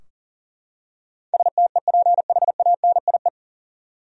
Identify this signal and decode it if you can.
Morse code, heard as a keyed tone, about 40 wpm, 700 Hz. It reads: STEPHANIE
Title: USA Common Names - 40wpm